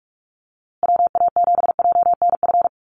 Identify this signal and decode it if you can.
Morse code, heard as a keyed tone, about 44 wpm, 700 Hz. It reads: WA7JNF
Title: USA Call Signs Repeated 3x 45wpm